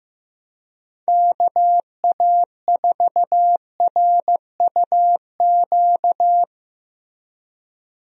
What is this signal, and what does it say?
Morse code, heard as a keyed tone, about 16 wpm, 700 Hz. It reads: KA4RUQ
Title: USA Call Signs - 6 Characters - Repeated 2x - 15wpm